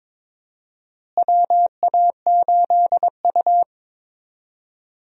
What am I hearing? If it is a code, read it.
Morse code, heard as a keyed tone, about 22 wpm, 700 Hz. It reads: WA8U
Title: USA Call Signs - 4 Characters - Repeated 2x - 22wpm